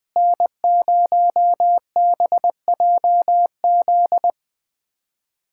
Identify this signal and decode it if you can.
Morse code, heard as a keyed tone, about 19 wpm, 700 Hz. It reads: N0BJZ
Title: USA Callsign DE Callsign - 5 Characters - 20wpm